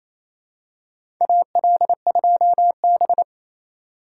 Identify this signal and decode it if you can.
Morse code, heard as a keyed tone, about 28 wpm, 700 Hz. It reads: AL2B
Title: USA Call Signs - 4 Characters - 2x Character Spacing - 28wpm